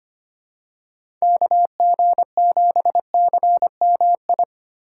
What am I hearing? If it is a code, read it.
Morse code, heard as a keyed tone, about 25 wpm, 700 Hz. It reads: KG7CMI
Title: USA Call Signs - 6 Characters - 2x Character Spacing - 25wpm